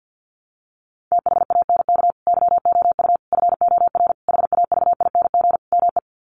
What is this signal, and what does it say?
Morse code, heard as a keyed tone, about 48 wpm, 700 Hz. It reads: THANK YOU FOR HAVING ME
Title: Sets of 5 Words Encore 50wpm